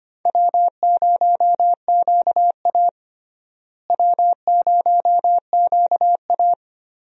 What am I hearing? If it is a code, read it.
Morse code, heard as a keyed tone, about 25 wpm, 700 Hz. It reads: W0QA W0QA
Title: USA Call Signs - 4 Characters - Repeated 3x - 25wpm